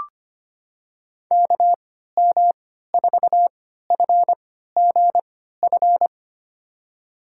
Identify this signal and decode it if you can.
Morse code, heard as a keyed tone, about 25 wpm, spaced as for 14 wpm, 700 Hz. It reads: KM4FGF
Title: USA Call Signs - 6 Characters - 3x Character Spacing - 25wpm